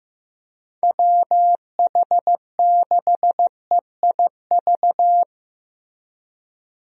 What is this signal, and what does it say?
Morse code, heard as a keyed tone, about 15 wpm, 700 Hz. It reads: WH6EIV
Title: USA Call Signs - 6 Characters - Repeated 3x - 15wpm